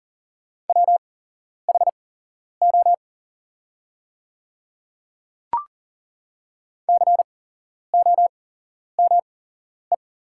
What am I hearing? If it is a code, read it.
Morse code, heard as a keyed tone, about 40 wpm, spaced as for 11 wpm, 700 Hz. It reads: WHO COME
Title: 8x Character Spacing - Top 100 Words - 40wpm